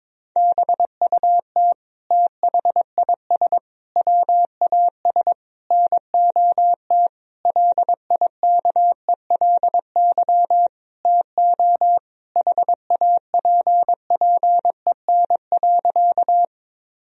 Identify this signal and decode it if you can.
Morse code, heard as a keyed tone, about 22 wpm, 700 Hz. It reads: BUT THIS WAS NOT LIKELY TO HAPPEN.
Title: Sentences from Top 1000 Words - Review Part 4 - 22wpm